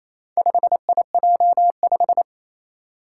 Morse code, heard as a keyed tone, about 27 wpm, 700 Hz. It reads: 5IJ5
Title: Four Challenging Letter-Number Combinations 28wpm